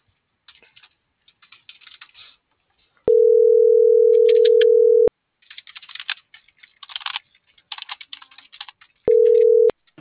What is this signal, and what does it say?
Ringback tone (North America), 440 + 480 Hz.